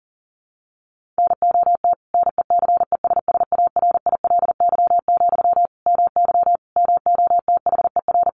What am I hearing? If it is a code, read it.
Morse code, heard as a keyed tone, about 40 wpm, 700 Hz. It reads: NOT NECESSARILY, MY MOTHER
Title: Sentences from Top 1400 Words 40wpm